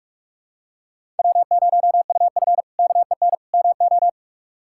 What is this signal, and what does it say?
Morse code, heard as a keyed tone, about 46 wpm, 700 Hz. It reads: W0UF KEN MO
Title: K1USN SST QSO Practice - 45wpm